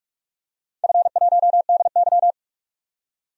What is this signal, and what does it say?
Morse code, heard as a keyed tone, about 41 wpm, 700 Hz. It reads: F1DY
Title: DXCC-100 Call Signs - 4 Characters - 45wpm